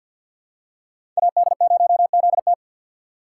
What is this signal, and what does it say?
Morse code, heard as a keyed tone, about 50 wpm, 700 Hz. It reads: AD0ZT
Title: USA Call Signs - 4x Character Spacing - 50wpm